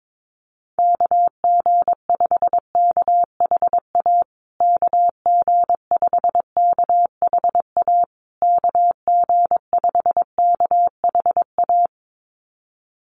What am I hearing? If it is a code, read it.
Morse code, heard as a keyed tone, about 22 wpm, 700 Hz. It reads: KG5KHA KG5KHA KG5KHA
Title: USA Call Signs Repeated 3x 22wpm